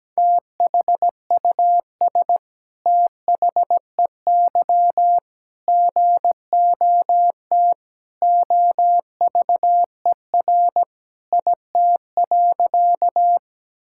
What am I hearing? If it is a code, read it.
Morse code, heard as a keyed tone, about 17 wpm, 700 Hz. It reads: THUS THEY GOT OVER IT.
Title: Sentences from Top 1000 Words - Review Part 5 - 17wpm